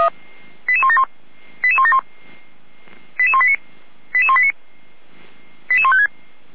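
Five-tone selective call (ZVEI), heard as quick runs of five tones, tones of about 70 ms. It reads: 88171 88171 88189 88189 88156
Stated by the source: ZVEI five-tone selective call received on an SDR, recorded by the Feuerwehr Orlamuende fire brigade.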